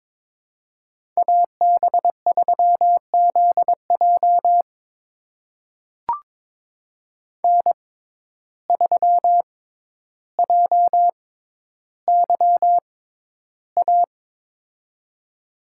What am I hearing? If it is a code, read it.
Morse code, heard as a keyed tone, about 21 wpm, spaced as for 11 wpm, 700 Hz. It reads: AB3ZJ N3JYA
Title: USA Call Signs - 5 Characters - 6x Character Spacing - 22wpm